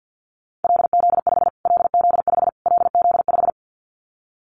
Morse code, heard as a keyed tone, about 50 wpm, 700 Hz. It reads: LZ5 LZ5 LZ5
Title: Callsign Prefixes for Top 25 DXCC - Repeated 3x - 50wpm